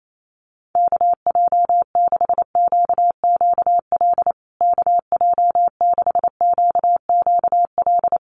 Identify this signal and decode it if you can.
Morse code, heard as a keyed tone, about 28 wpm, 700 Hz. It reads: KJ6QQL KJ6QQL
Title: USA Call Signs Repeated 3x 28wpm